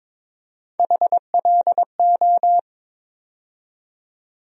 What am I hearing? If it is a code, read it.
Morse code, heard as a keyed tone, about 22 wpm, 700 Hz. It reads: HLO
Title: USA Call Sign Postfixes - 3 Letters 22wpm